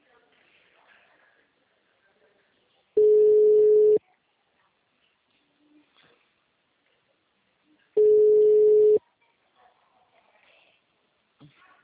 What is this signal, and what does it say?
Ringback tone (Europe, 425 Hz), with speech.